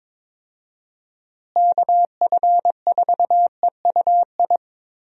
Morse code, heard as a keyed tone, about 22 wpm, 700 Hz. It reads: KF4EUI
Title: USA Call Signs - 6 Characters - Repeated 2x - 22wpm